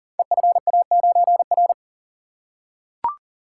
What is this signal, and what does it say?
Morse code, heard as a keyed tone, about 40 wpm, 690 Hz. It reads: EFA9R
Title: Code Groups - 5 Letter:Numbers - 40wpm